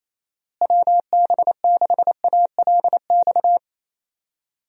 Morse code, heard as a keyed tone, about 28 wpm, 700 Hz. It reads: WB6ALX
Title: USA Call Signs - 6 Characters - Repeated 3x - 28wpm